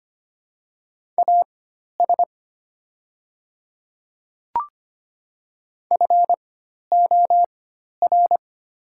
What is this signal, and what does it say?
Morse code, heard as a keyed tone, about 25 wpm, spaced as for 12 wpm, 700 Hz. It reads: AS FOR
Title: 4x Character Spacing - Top 100 Words - 25wpm